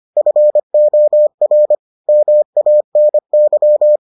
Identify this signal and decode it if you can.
Morse code, heard as a keyed tone, about 25 wpm, 600 Hz. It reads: FOR MANY